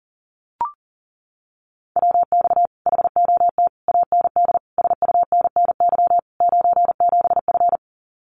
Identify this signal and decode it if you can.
Morse code, heard as a keyed tone, about 40 wpm, 700 Hz. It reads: WX HOT AND SUNNY 97F
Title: Common QSO Phrases Encore 40wpm